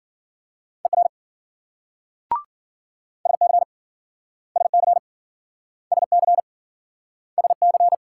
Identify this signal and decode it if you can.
Morse code, heard as a keyed tone, about 53 wpm, 700 Hz. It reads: ER SC SC SC SC